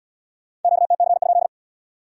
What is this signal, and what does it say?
Morse code, heard as a keyed tone, about 73 wpm, 700 Hz. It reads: QTC?